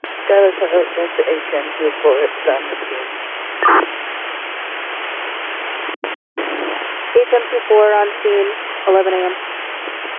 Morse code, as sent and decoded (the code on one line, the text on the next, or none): none